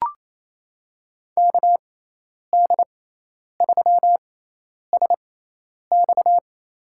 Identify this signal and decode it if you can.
Morse code, heard as a keyed tone, about 27 wpm, spaced as for 10 wpm, 700 Hz. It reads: KD3SX
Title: USA Call Signs - 5 Characters - 6x Character Spacing - 28wpm